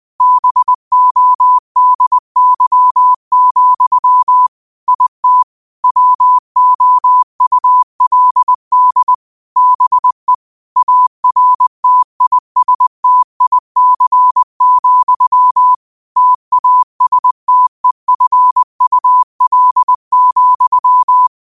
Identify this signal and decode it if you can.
Morse code, heard as a keyed tone, about 20 wpm, 1000 Hz. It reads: BODY, IT WOULD BE ARTISTIC, TASTEFUL,